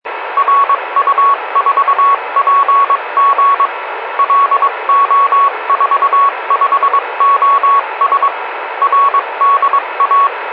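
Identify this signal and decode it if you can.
Morse code, heard as a keyed tone, about 22 wpm, 1120 Hz, with radio noise. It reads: RU4PG LO45OS RDA
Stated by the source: Morse signal on 144.375 from station RU4PG, locator LO45OS, RDA district TA07.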